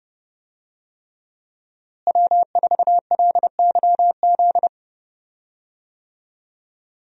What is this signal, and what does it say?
Morse code, heard as a keyed tone, about 28 wpm, 700 Hz. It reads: W4LYZ
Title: USA Call Signs - 5 Characters - Repeated 3x - 30wpm